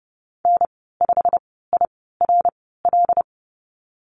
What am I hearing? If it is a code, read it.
Morse code, heard as a keyed tone, about 30 wpm, spaced as for 17 wpm, 700 Hz. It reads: N5IRL
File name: Lesson 19g - Letter w - Call Signs 30wpm